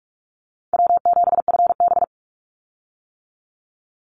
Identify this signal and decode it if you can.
Morse code, heard as a keyed tone, about 44 wpm, 700 Hz. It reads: W7FB
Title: Call Signs - 45wpm